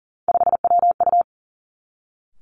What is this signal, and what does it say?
Morse code, heard as a keyed tone, about 40 wpm, 700 Hz. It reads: 5WU
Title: Three Challenging Letter-Number Combinations 40wpm